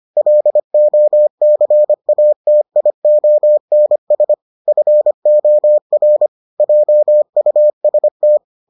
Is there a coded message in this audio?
Yes: Morse code from 0.2 to 8.4 s, heard as a keyed tone, about 25 wpm, 600 Hz. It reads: LOCATIONS FOR JUST